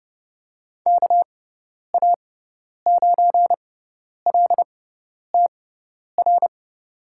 Morse code, heard as a keyed tone, about 32 wpm, spaced as for 10 wpm, 700 Hz. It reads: KA9LTR
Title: USA Call Signs - 6 Characters - 6x Character Spacing - 30wpm